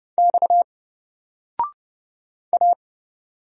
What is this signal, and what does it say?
Morse code, heard as a keyed tone, about 29 wpm, 700 Hz. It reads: X A
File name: Single-Letter-Number-30wpm